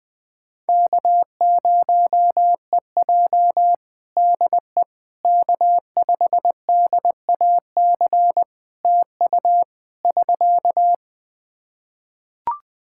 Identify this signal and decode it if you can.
Morse code, heard as a keyed tone, about 21 wpm, 700 Hz. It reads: K0EJ DE K5DAC TU <SK>
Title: Practice QSO - Exchange 4 - Ending - 20wpm